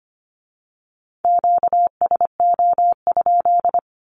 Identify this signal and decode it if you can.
Morse code, heard as a keyed tone, about 25 wpm, 700 Hz. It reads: QSO?
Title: Q-Codes with Spoken Meaning - 25wpm